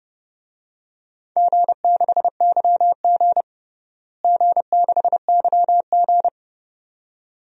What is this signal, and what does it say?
Morse code, heard as a keyed tone, about 30 wpm, 700 Hz. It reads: G6YG G6YG
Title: DXCC-100 Call Signs - 4 Characters - Repeated 2x - 30wpm